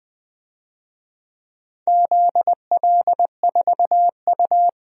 Morse code, heard as a keyed tone, about 20 wpm, 700 Hz. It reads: ZL4U
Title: DXCC-100 Call Signs - 4 Characters - 20wpm